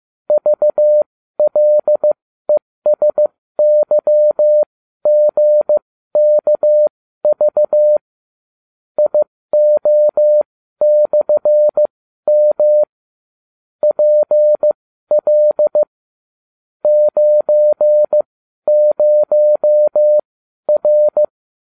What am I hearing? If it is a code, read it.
Morse code, heard as a keyed tone, about 15 wpm, spaced as for 12 wpm, 600 Hz. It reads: VLESYGKV IO/M PL 90R